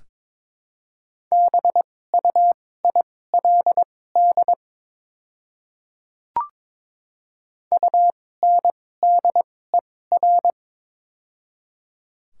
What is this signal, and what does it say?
Morse code, heard as a keyed tone, about 21 wpm, spaced as for 16 wpm, 700 Hz. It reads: BUILD UNDER
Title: 2x Character Spacing - Top 200 Words - 22wpm